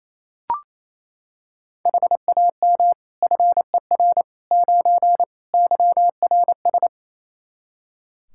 Morse code, heard as a keyed tone, about 27 wpm, 700 Hz. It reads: HAM FER 9 YRS
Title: K1USN SST QSO Practice 28wpm